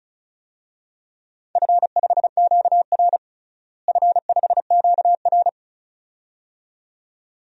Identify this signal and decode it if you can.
Morse code, heard as a keyed tone, about 36 wpm, 700 Hz. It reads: F5QR F5QR
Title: DXCC-100 Call Signs - 4 Characters - Repeated 2x - 35wpm